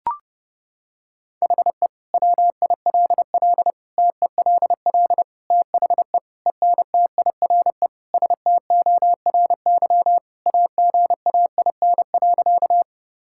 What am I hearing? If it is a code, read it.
Morse code, heard as a keyed tone, about 32 wpm, 700 Hz. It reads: HE WILL TELL THE ENTIRE STORY AGAIN.
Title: Sentences from Top 1000 Words - Review Part 5 - 30wpm